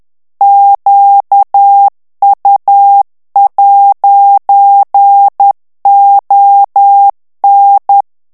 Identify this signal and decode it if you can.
Morse code, heard as a keyed tone, about 10 wpm, 800 Hz. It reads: QU'ON
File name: FRANCAIS TEXTE CLAIR 2 VITESSE 10 MOTS MINUTES